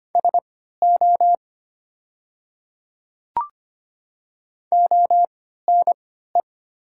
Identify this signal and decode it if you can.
Morse code, heard as a keyed tone, about 25 wpm, spaced as for 14 wpm, 700 Hz. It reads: SO ONE
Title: 3x Character Spacing - Top 100 Words - 25wpm